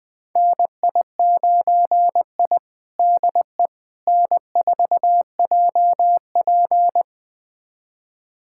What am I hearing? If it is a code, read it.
Morse code, heard as a keyed tone, about 21 wpm, 700 Hz. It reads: NI9I DE N4JP
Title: USA Callsign DE Callsign - 4 Characters - 20wpm